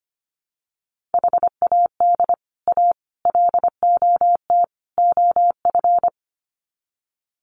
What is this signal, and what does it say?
Morse code, heard as a keyed tone, about 25 wpm, 700 Hz. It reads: HAD A LOT OF
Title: Sets of 4 Words 25wpm